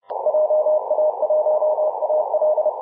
Morse code, heard as a keyed tone, about 31 wpm, 610 Hz, with noise. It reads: JA1AL
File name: dec-27-2259z-vk9-g7vjr-21-039mhz-cw